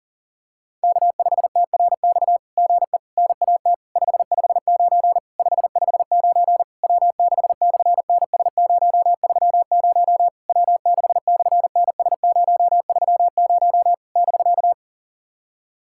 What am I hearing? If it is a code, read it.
Morse code, heard as a keyed tone, about 40 wpm, 690 Hz. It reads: K5TRX GE NAT 559 559 W6/NS030 W6/NS030 <BK>
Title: SOTA QSO Practice 40wpm